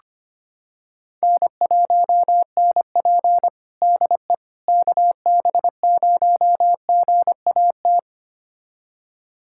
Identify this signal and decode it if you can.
Morse code, heard as a keyed tone, about 25 wpm, 700 Hz. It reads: N1NP DE KB0GAT
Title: USA Callsign DE Callsign 25wpm